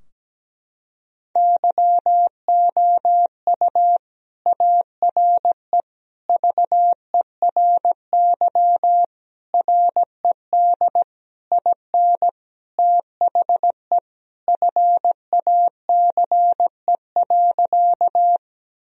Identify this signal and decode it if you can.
Morse code, heard as a keyed tone, about 17 wpm, 700 Hz. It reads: YOU ARE VERY RED IN THE FACE.
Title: Sentences from Top 1000 Words - Review Part 2 - 17wpm